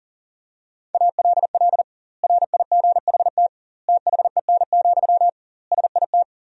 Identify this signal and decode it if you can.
Morse code, heard as a keyed tone, about 37 wpm, 700 Hz. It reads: ALL RIGHT THEN, SIT
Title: 2x Word-Spacing - Sentences from Top 500 Words - 40wpm